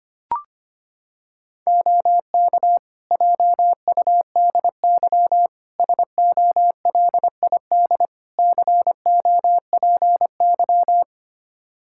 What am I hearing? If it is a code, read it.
Morse code, heard as a keyed tone, about 25 wpm, 700 Hz. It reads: OK JUDY SOLID COPY
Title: Practice QSO - Exchange 3 - More Info - 25wpm